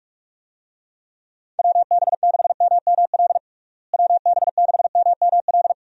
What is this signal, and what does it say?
Morse code, heard as a keyed tone, about 46 wpm, 700 Hz. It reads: WB6MML WB6MML
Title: USA Call Signs - 6 Characters - Repeated 3x - 45wpm